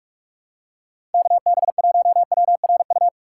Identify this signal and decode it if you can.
Morse code, heard as a keyed tone, about 46 wpm, 700 Hz. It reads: KB1WRU
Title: USA Call Signs - 6 Characters - 4x Character Spacing - 45wpm